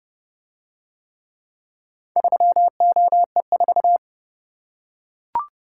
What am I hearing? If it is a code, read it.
Morse code, heard as a keyed tone, about 30 wpm, 700 Hz. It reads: 3OE4